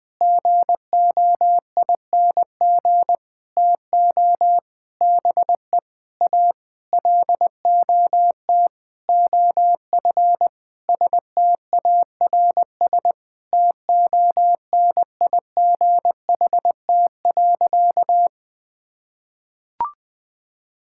Morse code, heard as a keyed tone, about 20 wpm, 700 Hz. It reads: GOING TO BE A LOT OF STARS TONIGHT.
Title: Sentences with 8-Word Phrases - 20wpm